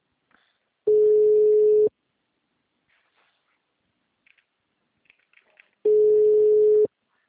Ringback tone (Europe, 425 Hz).
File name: out- -2031- - - .2086